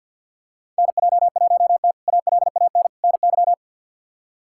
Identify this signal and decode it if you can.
Morse code, heard as a keyed tone, about 47 wpm, 700 Hz. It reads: NJ1T ALAN NY
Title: K1USN SST QSO Practice - 50wpm